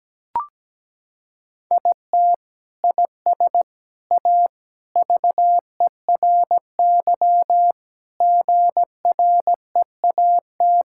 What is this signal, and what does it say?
Morse code, heard as a keyed tone, about 17 wpm, 700 Hz. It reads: IT IS A VERY GREAT
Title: Sentences from Top 1000 Words - Review Part 5 - 17wpm